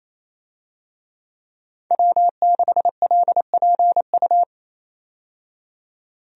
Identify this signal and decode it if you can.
Morse code, heard as a keyed tone, about 28 wpm, 700 Hz. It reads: W6LPU
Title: USA Call Signs - 5 Characters - 28wpm